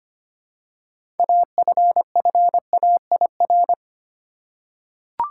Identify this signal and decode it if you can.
Morse code, heard as a keyed tone, about 25 wpm, 700 Hz. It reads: AFFAIR
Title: Six Letter Words - 25wpm